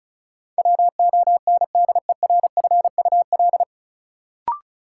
Morse code, heard as a keyed tone, about 36 wpm, 700 Hz. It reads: WONDERFUL
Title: Sentences from Top 1100 Words - New Word Drill Alt - 35wpm